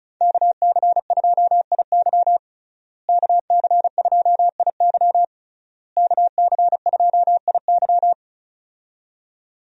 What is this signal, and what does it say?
Morse code, heard as a keyed tone, about 36 wpm, 700 Hz. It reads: KC2IY KC2IY KC2IY
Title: USA Call Signs - 5 Characters - Repeated 3x - 35wpm